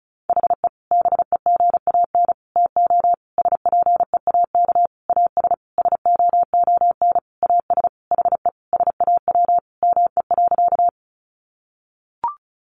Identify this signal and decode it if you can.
Morse code, heard as a keyed tone, about 34 wpm, 700 Hz. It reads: HE BEGAN TO SPEAK AS SOON AS HE SAW ME.
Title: Sentences from Top 1100 Words 35wpm